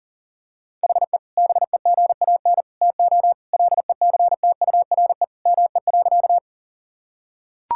Morse code, heard as a keyed tone, about 40 wpm, 700 Hz. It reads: HE BEGAN TO LECTURE ME.
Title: Sentences from Top 1600 Words 40wpm